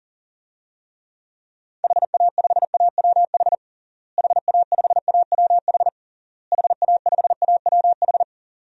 Morse code, heard as a keyed tone, about 40 wpm, 700 Hz. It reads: HA5AWH HA5AWH HA5AWH
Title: DXCC-100 Call Signs - 6 Characters - Repeated 3x - 40wpm